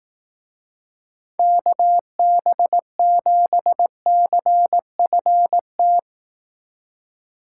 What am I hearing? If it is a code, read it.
Morse code, heard as a keyed tone, about 18 wpm, 700 Hz. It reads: KB7CFT
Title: Speed Racing - USA 2x3 Call Signs 15wpm